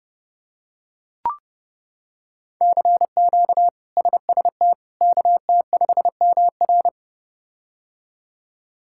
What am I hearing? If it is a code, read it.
Morse code, heard as a keyed tone, about 30 wpm, 700 Hz. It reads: CQ SST KT5MR
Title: K1USN SST QSO Practice - 30wpm